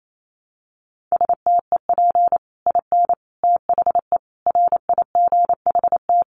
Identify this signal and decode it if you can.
Morse code, heard as a keyed tone, about 28 wpm, 700 Hz. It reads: STEP IN THE RIGHT
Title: Sets of 5 Words Encore 28wpm